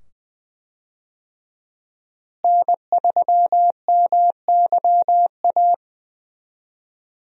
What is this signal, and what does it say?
Morse code, heard as a keyed tone, about 20 wpm, 700 Hz. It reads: N3MYA
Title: USA Call Signs - 5 Characters - Repeated 2x - 20wpm